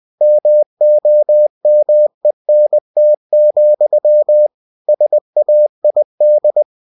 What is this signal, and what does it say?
Morse code, heard as a keyed tone, about 20 wpm, 600 Hz. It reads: MOMENT, SAID
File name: cw-books / winnie-the-pooh / winnie-20wpm-page-0030